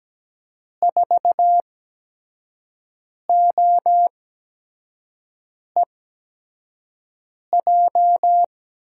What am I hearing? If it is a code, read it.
Morse code, heard as a keyed tone, about 17 wpm, spaced as for 5 wpm, 700 Hz. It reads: 4OEJ
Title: USA Call Signs - 5 Characters - 8x Character Spacing - 17wpm